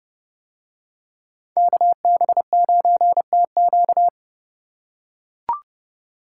Morse code, heard as a keyed tone, about 28 wpm, 700 Hz. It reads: KB9TQ